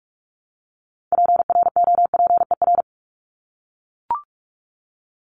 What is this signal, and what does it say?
Morse code, heard as a keyed tone, about 44 wpm, 700 Hz. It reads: PROPER
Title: Top 751-1000 Words 45wpm